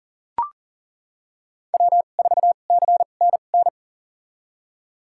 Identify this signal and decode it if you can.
Morse code, heard as a keyed tone, about 39 wpm, spaced as for 28 wpm, 700 Hz. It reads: W4CNN
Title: USA Call Signs - 5 Characters - 2x Character Spacing - 40wpm